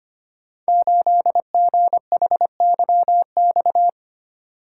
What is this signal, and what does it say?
Morse code, heard as a keyed tone, about 25 wpm, 700 Hz. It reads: 8GHYX
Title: Code Groups - 5 Letter:Numbers - 25wpm